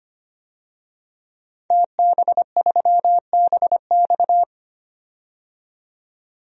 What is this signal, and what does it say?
Morse code, heard as a keyed tone, about 25 wpm, 700 Hz. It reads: TB3BX
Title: DXCC-100 Call Signs - 5 Characters - 25wpm